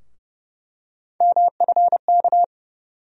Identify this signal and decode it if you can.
Morse code, heard as a keyed tone, about 30 wpm, 700 Hz. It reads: MFK
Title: USA Call Sign Postfixes - 3 Letters 30wpm